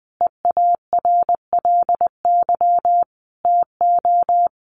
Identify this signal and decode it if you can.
Morse code, heard as a keyed tone, about 20 wpm, 700 Hz. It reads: EARLY TO
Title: Sentences from Top 700 Words 20wpm